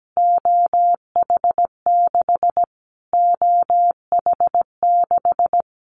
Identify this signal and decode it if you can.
Morse code, heard as a keyed tone, about 17 wpm, 700 Hz. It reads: OH6 OH6
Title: Callsign Prefixes for Top 25 DXCC - Repeated 3x - 17wpm